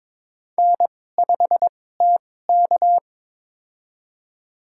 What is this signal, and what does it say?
Morse code, heard as a keyed tone, about 21 wpm, spaced as for 16 wpm, 700 Hz. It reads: N5TK
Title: USA Call Signs - 4 Characters - 2x Character Spacing - 22wpm